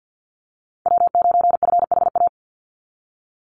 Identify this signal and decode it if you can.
Morse code, heard as a keyed tone, about 49 wpm, 700 Hz. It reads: W9FHA
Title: NA SOTA Chaser Callsigns 50wpm